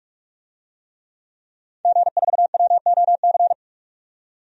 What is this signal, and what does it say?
Morse code, heard as a keyed tone, about 46 wpm, 700 Hz. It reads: G4WOC
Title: DXCC-100 Call Signs - 5 Characters - 45wpm